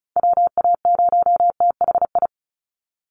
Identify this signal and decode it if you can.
Morse code, heard as a keyed tone, about 34 wpm, 700 Hz. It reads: WA0THI
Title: USA Call Signs 35wpm